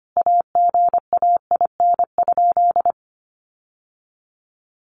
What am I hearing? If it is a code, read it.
Morse code, heard as a keyed tone, about 25 wpm, 700 Hz. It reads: AGAIN?
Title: Sentences from Top 1200 Words 25wpm